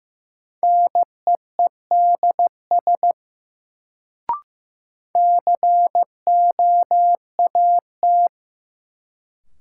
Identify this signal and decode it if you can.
Morse code, heard as a keyed tone, about 15 wpm, 700 Hz. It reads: NEEDS COAT
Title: Sentences from Top 1600 Words - New Word Drill Alt - 15wpm